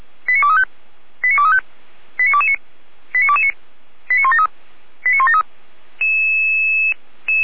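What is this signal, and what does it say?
Five-tone selective call (ZVEI), heard as quick runs of five tones, tones of about 80 ms. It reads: 89235 89235 89229 89229 89173 89173
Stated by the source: ZVEI five-tone selective call received by SDR, from the audio server of Feuerwehr Orlamünde (fire brigade).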